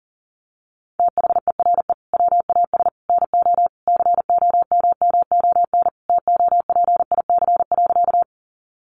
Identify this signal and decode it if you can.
Morse code, heard as a keyed tone, about 40 wpm, 700 Hz. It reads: THERE WAS NO COMMON TOPIC.
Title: Sentences from Top 1200 Words 40wpm